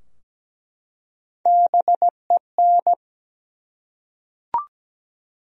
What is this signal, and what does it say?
Morse code, heard as a keyed tone, about 17 wpm, 700 Hz. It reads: BEN
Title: USA Common Names - Max 3 Letters - 17wpm